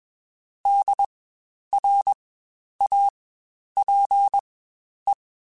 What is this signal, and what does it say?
Morse code, heard as a keyed tone, about 20 wpm, spaced as for 10 wpm, 800 Hz. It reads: DRAPE
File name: Spaced Alphas / Spaced Alphas mp3s / Ds sp @20